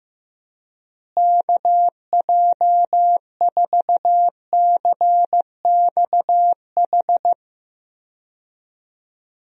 Morse code, heard as a keyed tone, about 15 wpm, 700 Hz. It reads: KJ4CXH
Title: USA Call Signs - 6 Characters - Repeated 3x - 15wpm